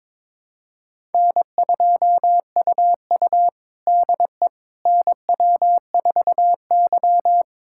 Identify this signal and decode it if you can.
Morse code, heard as a keyed tone, about 22 wpm, 700 Hz. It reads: N2UU DE NW4Y
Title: USA Callsign DE Callsign - 4 Characters - 22wpm